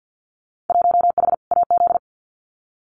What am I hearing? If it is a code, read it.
Morse code, heard as a keyed tone, about 50 wpm, 700 Hz. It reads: 1H AZ
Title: Winter Field Day Exchanges 50wpm